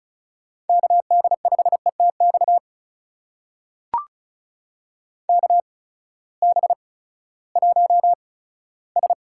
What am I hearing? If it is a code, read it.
Morse code, heard as a keyed tone, about 36 wpm, 700 Hz. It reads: K D 5 E T X K B 1 S
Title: USA Call Signs - 6 Characters - 8x Character Spacing - 35wpm